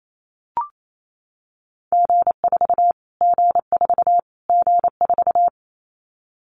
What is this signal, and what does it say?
Morse code, heard as a keyed tone, about 28 wpm, 700 Hz. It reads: G4 G4 G4
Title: Callsign Prefixes for Top 25 DXCC - Repeated 3x - 28wpm